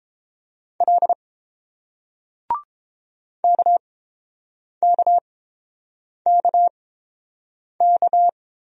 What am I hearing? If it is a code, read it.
Morse code, heard as a keyed tone, about 29 wpm, spaced as for 7 wpm, 700 Hz. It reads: L KKKK